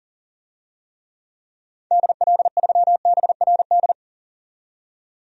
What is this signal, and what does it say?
Morse code, heard as a keyed tone, about 39 wpm, 700 Hz. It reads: DL3BRD
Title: DXCC-100 Call Signs - 6 Characters - 40wpm